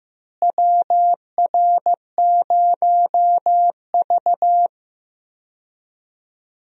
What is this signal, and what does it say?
Morse code, heard as a keyed tone, about 15 wpm, 700 Hz. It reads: WR0V
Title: USA Call Signs - 4 Characters - Repeated 2x - 15wpm